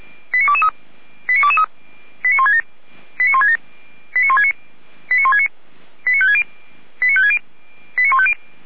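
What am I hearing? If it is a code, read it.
Five-tone selective call (ZVEI), heard as quick runs of five tones, tones of about 70 ms. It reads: 89223 89223 89167 89167 89169 89169 89560
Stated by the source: Five-tone ZVEI selective call received with an SDR.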